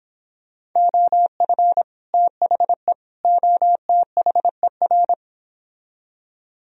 Morse code, heard as a keyed tone, about 26 wpm, 700 Hz. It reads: OF THE OTHER